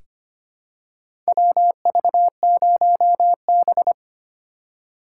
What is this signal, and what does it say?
Morse code, heard as a keyed tone, about 25 wpm, 700 Hz. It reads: WV0B
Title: USA Call Signs - 4 Characters - 3x Character Spacing - 25wpm